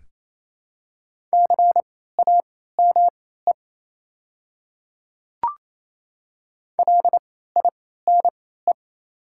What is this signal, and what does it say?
Morse code, heard as a keyed tone, about 27 wpm, spaced as for 16 wpm, 700 Hz. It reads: CAME LINE
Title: 3x Character Spacing - Top 300 Words - 28wpm